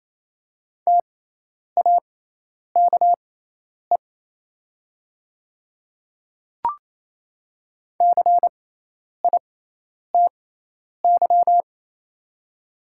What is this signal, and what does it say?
Morse code, heard as a keyed tone, about 28 wpm, spaced as for 10 wpm, 700 Hz. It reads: TAKE CITY
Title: 6x Character Spacing - Top 300 Words - 28wpm